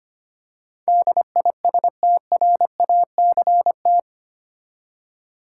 Morse code, heard as a keyed tone, about 25 wpm, 700 Hz. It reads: DISTRACT
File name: Eight Letter Words - 25wpm